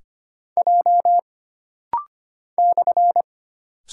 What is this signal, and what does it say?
Morse code, heard as a keyed tone, about 25 wpm, 700 Hz. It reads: J /
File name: Single Character (Letter:Number:Punctuation:Prosign) - Rapid-Fire - 25wpm